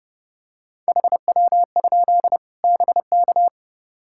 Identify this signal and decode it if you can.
Morse code, heard as a keyed tone, about 30 wpm, 700 Hz. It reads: HW? BK
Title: Practice QSO - Exchange 1 - Beginning - 30wpm